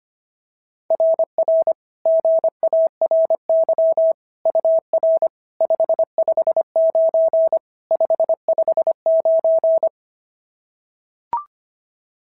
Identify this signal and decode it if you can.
Morse code, heard as a keyed tone, about 25 wpm, 690 Hz. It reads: RR GARY UR 559 559
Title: SOTA QSO Practice 25wpm